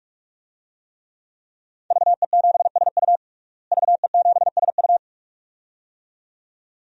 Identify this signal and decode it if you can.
Morse code, heard as a keyed tone, about 46 wpm, 700 Hz. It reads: VE7SU VE7SU
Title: DXCC-100 Call Signs - 5 Characters - Repeated 2x - 45wpm